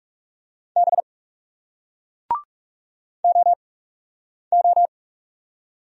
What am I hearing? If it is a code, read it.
Morse code, heard as a keyed tone, about 41 wpm, spaced as for 8 wpm, 700 Hz. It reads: B OO